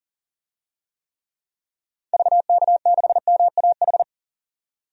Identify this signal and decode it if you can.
Morse code, heard as a keyed tone, about 40 wpm, 700 Hz. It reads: VK6MAH
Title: DXCC-100 Call Signs - 6 Characters - 40wpm